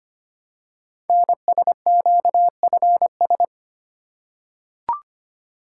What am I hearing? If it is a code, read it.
Morse code, heard as a keyed tone, about 25 wpm, 700 Hz. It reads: NSQFS
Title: Code Groups - 5 Letters - 25wpm